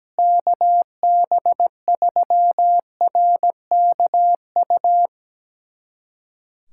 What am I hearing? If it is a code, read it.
Morse code, heard as a keyed tone, about 17 wpm, 700 Hz. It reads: KB3RKU
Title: USA Callsign DE Callsign 17wpm